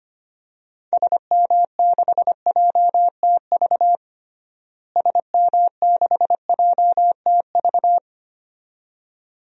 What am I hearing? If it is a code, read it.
Morse code, heard as a keyed tone, about 25 wpm, 700 Hz. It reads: SM6JTV SM6JTV
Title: DXCC-100 Call Signs - 6 Characters - Repeated 2x - 25wpm